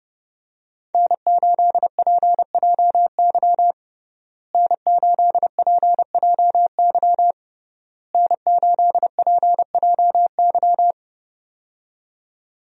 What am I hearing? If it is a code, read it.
Morse code, heard as a keyed tone, about 30 wpm, 700 Hz. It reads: N8PJY N8PJY N8PJY
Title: USA Call Signs - 5 Characters - Repeated 3x - 30wpm